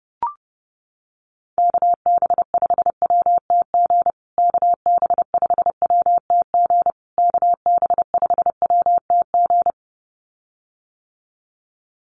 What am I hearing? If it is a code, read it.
Morse code, heard as a keyed tone, about 30 wpm, 700 Hz. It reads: KB5WTG KB5WTG KB5WTG
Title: USA Call Signs Repeated 3x 30wpm